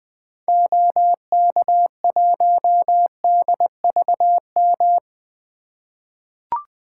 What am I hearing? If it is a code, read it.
Morse code, heard as a keyed tone, about 21 wpm, 700 Hz. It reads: OK1DVM
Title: 2023 SOTA Activator Call Signs - 20wpm